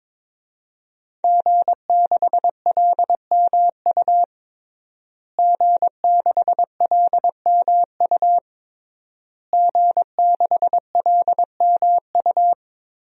Morse code, heard as a keyed tone, about 22 wpm, 700 Hz. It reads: G6LMU G6LMU G6LMU
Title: DXCC-100 Call Signs - 5 Characters - Repeated 3x - 22wpm